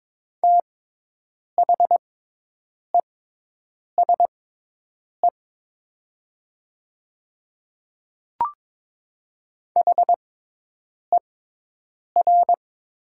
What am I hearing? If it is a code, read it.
Morse code, heard as a keyed tone, about 22 wpm, spaced as for 8 wpm, 700 Hz. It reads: THESE HER
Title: 6x Character Spacing - Top 100 Words - 22wpm